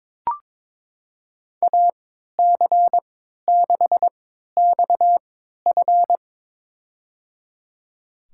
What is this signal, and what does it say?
Morse code, heard as a keyed tone, about 23 wpm, spaced as for 13 wpm, 700 Hz. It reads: AC6XF
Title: USA Call Signs - 5 Characters - 3x Character Spacing - 22wpm